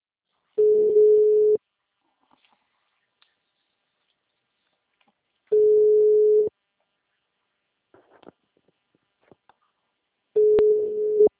Ringback tone (Europe, 425 Hz).
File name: out- -2012- - - .73461